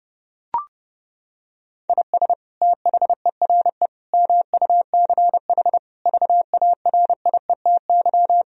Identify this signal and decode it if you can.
Morse code, heard as a keyed tone, about 30 wpm, 700 Hz. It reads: IS THERE MUCH VARIETY
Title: Sentences from Top 1000 Words - Review Part 4 - 30wpm